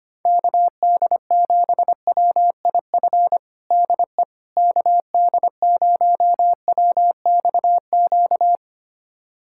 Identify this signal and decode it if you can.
Morse code, heard as a keyed tone, about 25 wpm, 700 Hz. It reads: KD7WIF DE KD0WXQ
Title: USA Callsign DE Callsign - 6 Characters - 25wpm